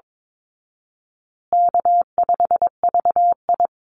Morse code, heard as a keyed tone, about 22 wpm, 700 Hz. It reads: K5VI
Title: Easy Call Signs Repeated 3x - 22wpm